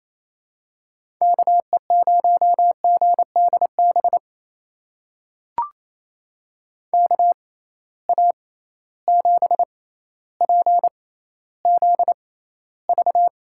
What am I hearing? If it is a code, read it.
Morse code, heard as a keyed tone, about 28 wpm, spaced as for 14 wpm, 700 Hz. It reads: KE0GDB KA7PZV
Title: USA Call Signs - 6 Characters - 6x Character Spacing - 28wpm